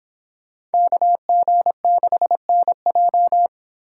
Morse code, heard as a keyed tone, about 29 wpm, 700 Hz. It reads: KG6NJ